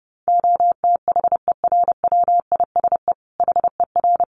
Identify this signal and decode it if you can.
Morse code, heard as a keyed tone, about 30 wpm, 700 Hz. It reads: OTHERWISE HER
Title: Sentences from Top 1200 Words 30wpm